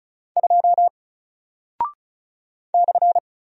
Morse code, heard as a keyed tone, about 36 wpm, 700 Hz. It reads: 2 /
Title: Single Character (Letter:Number:Punctuation:Prosign) Alt 35wpm